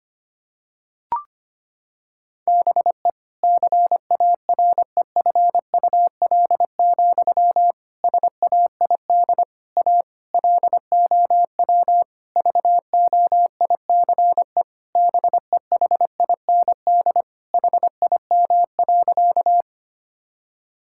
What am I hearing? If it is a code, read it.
Morse code, heard as a keyed tone, about 25 wpm, 700 Hz. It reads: BE CAREFUL, SAID A LOW VOICE BEHIND HIM.
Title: Sentences from Top 1600 Words 25wpm